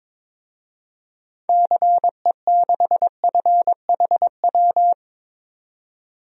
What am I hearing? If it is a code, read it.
Morse code, heard as a keyed tone, about 22 wpm, 700 Hz. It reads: CE6FHW
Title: DXCC-100 Call Signs - 6 Characters - 22wpm